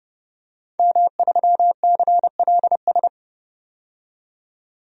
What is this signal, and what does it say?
Morse code, heard as a keyed tone, about 30 wpm, 700 Hz. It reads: M3CLS
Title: DXCC-100 Call Signs - 5 Characters - 30wpm